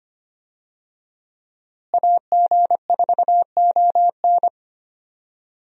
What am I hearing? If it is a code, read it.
Morse code, heard as a keyed tone, about 25 wpm, 700 Hz. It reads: AG4ON
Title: USA Call Signs - 5 Characters - Repeated 2x - 25wpm